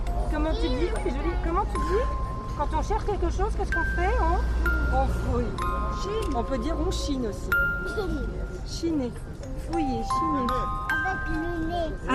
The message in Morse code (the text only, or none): none